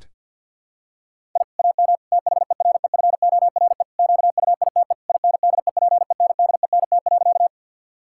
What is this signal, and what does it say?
Morse code, heard as a keyed tone, about 50 wpm, 700 Hz. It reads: I AM THEREFORE QUITE INDEPENDENT.
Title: Sentences from Top 1000 Words - Review Part 6 - 50wpm